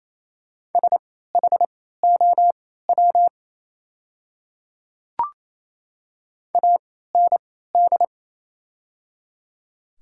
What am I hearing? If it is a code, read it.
Morse code, heard as a keyed tone, about 27 wpm, spaced as for 16 wpm, 690 Hz. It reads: SHOW AND
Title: 3x Character Spacing - Top 200 Words - 28wpm